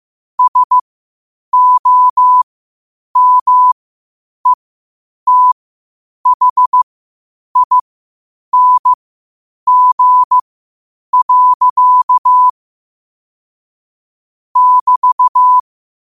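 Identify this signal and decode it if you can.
Morse code, heard as a keyed tone, about 15 wpm, spaced as for 8 wpm, 1000 Hz. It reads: SOMETHING. <BT>
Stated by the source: Morse code quotes at 8 WPM.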